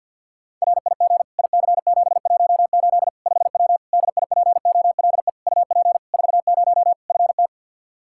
Morse code, heard as a keyed tone, about 49 wpm, 700 Hz. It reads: RIG IC718 5W DIPOLE UP 40 FT
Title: K1USN SST QSO Practice 50wpm